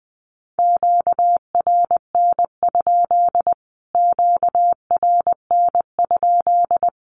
Morse code, heard as a keyed tone, about 19 wpm, 700 Hz. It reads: QRN? QRN?
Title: Q-Codes Repeated 3x 20wpm